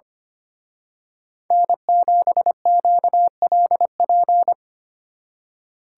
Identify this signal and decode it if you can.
Morse code, heard as a keyed tone, about 29 wpm, 700 Hz. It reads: N7QLP
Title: Speed Racing - USA 1x3 Call Signs 25wpm